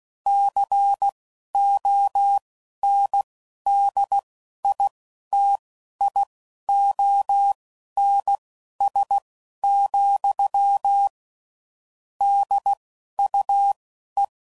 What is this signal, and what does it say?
Morse code, heard as a keyed tone, about 16 wpm, spaced as for 11 wpm, 800 Hz. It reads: CONDITIONS, DUE
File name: CW Code Groups @10WPM/Technical Subjects